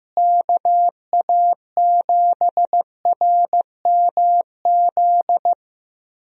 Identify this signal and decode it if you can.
Morse code, heard as a keyed tone, about 15 wpm, 700 Hz. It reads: KA7RMZ
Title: USA Call Signs - 6 Characters - Repeated 2x - 15wpm